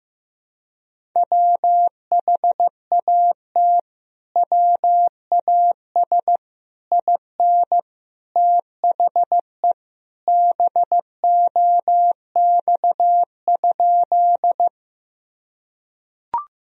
Morse code, heard as a keyed tone, about 15 wpm, 700 Hz. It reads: WHAT WAS IN THE BOX?
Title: Sentences from Top 1000 Words - Review Part 2 - 15wpm